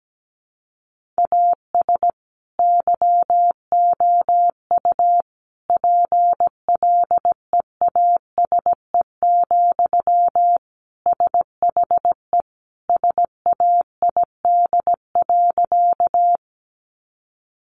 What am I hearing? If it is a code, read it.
Morse code, heard as a keyed tone, about 17 wpm, 700 Hz. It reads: AS YOU PLEASE, SHE SAID.
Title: Sentences from Top 1500 Words 17wpm